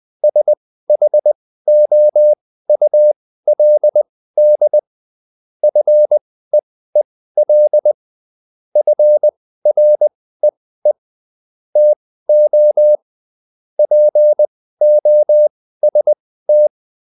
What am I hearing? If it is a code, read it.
Morse code, heard as a keyed tone, about 20 wpm, spaced as for 14 wpm, 600 Hz. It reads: SHOULD FEEL FREE TO POST